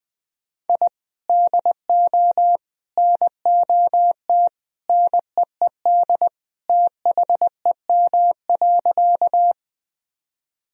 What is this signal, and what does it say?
Morse code, heard as a keyed tone, about 20 wpm, 700 Hz. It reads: I DO NOT NEED THEM.
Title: Sentences from Top 1000 Words - Review Part 1 - 20wpm